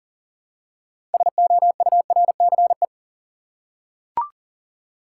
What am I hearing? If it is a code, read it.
Morse code, heard as a keyed tone, about 40 wpm, 700 Hz. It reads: SOURCE
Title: Six Letter Words - 40wpm